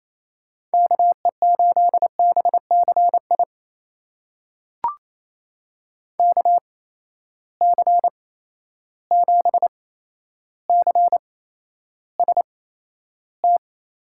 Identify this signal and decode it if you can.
Morse code, heard as a keyed tone, about 28 wpm, spaced as for 12 wpm, 700 Hz. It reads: KE8BCI KC7CST
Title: USA Call Signs - 6 Characters - 8x Character Spacing - 28wpm